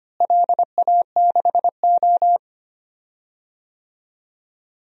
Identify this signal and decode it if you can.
Morse code, heard as a keyed tone, about 25 wpm, 700 Hz. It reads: LA6O
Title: DXCC-100 Call Signs - 4 Characters - Repeated 3x - 25wpm